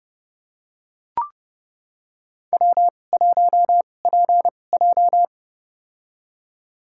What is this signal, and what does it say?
Morse code, heard as a keyed tone, about 31 wpm, spaced as for 22 wpm, 700 Hz. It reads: W1PJ
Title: USA Call Signs - 4 Characters - 2x Character Spacing - 30wpm